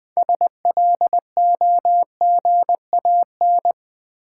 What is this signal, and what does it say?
Morse code, heard as a keyed tone, about 20 wpm, 700 Hz. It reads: SLOGAN
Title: 6 Letter Words - First 3 Letters Given - 20wpm